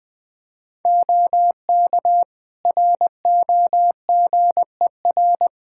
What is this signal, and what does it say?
Morse code, heard as a keyed tone, about 21 wpm, 700 Hz. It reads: OK ROGER
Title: Practice QSO - Exchange 4 - Ending - 20wpm